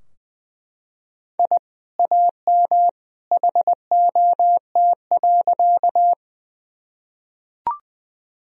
Morse code, heard as a keyed tone, about 20 wpm, 700 Hz. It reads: I AM HOT.
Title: Sentences from Top 1000 Words - Review Part 3 - 20wpm